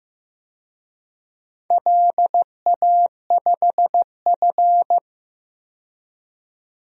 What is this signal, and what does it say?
Morse code, heard as a keyed tone, about 15 wpm, 700 Hz. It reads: LA5F
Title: DXCC-100 Call Signs - 4 Characters - 15wpm